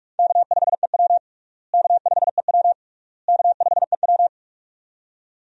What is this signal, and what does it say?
Morse code, heard as a keyed tone, about 42 wpm, 700 Hz. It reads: K5EW K5EW K5EW
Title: USA Call Signs - 4 Characters - Repeated 3x - 45wpm